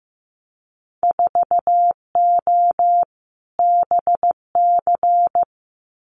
Morse code, heard as a keyed tone, about 15 wpm, 700 Hz. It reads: 4O BC
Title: Winter Field Day Exchanges Encore 15wpm